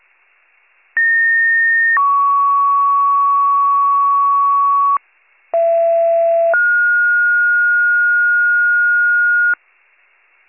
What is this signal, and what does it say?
Two-tone page, heard as a one-second tone, then a three-second tone, about 1850 then 1120 Hz, about 670 then 1430 Hz. It is heard from 1.0 to 9.5 s.